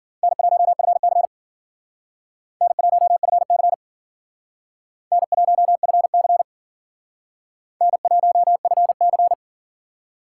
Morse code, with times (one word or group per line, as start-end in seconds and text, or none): 0.2-1.2 N1FC
2.6-3.7 N1FC
5.1-6.4 N1FC
7.8-9.3 N1FC